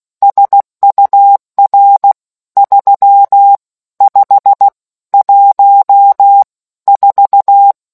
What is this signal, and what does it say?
Morse code, heard as a keyed tone, about 16 wpm, 800 Hz. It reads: SUR 3 5 1 4